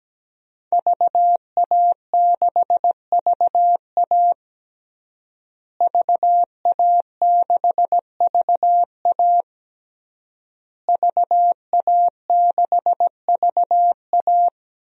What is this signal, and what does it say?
Morse code, heard as a keyed tone, about 17 wpm, 700 Hz. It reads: VA6VA VA6VA VA6VA
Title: DXCC-100 Call Signs - 5 Characters - Repeated 3x - 17wpm